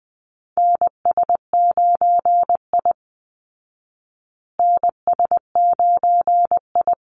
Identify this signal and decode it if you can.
Morse code, heard as a keyed tone, about 20 wpm, 700 Hz. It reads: NS9I NS9I
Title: Call Signs Repeated 3x - 20wpm